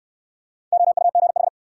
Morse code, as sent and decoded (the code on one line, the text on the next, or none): --- ..- --. ....
OUGH